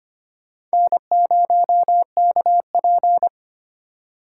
Morse code, heard as a keyed tone, about 34 wpm, spaced as for 30 wpm, 700 Hz. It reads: N0KP
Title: Speed Racing - USA 1x2 Call Signs 25wpm